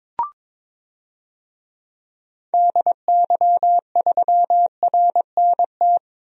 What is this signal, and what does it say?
Morse code, heard as a keyed tone, about 22 wpm, 700 Hz. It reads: DY3RNT
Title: DXCC-100 Call Signs - 6 Characters - 22wpm